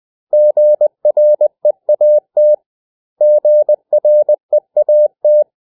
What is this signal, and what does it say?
Morse code, heard as a keyed tone, about 20 wpm, 600 Hz. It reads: GREAT GREAT
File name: cw / 100 Common English Words